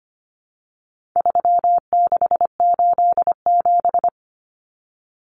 Morse code, heard as a keyed tone, about 25 wpm, 700 Hz. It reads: 3687
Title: Numbers - 4-Digit 25wpm